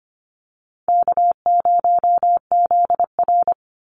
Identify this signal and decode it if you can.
Morse code, heard as a keyed tone, about 25 wpm, 700 Hz. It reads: K0ZR
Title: Easy Call Signs Repeated 2x - 25wpm